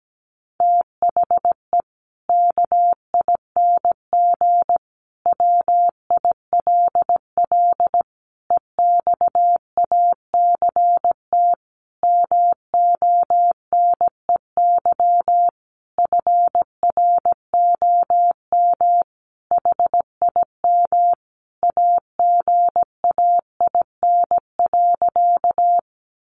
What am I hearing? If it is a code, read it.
Morse code, heard as a keyed tone, about 17 wpm, 700 Hz. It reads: THE KING WILL EXACT MONEY FROM HIM AGAIN.
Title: Sentences from Top 1300 Words 17wpm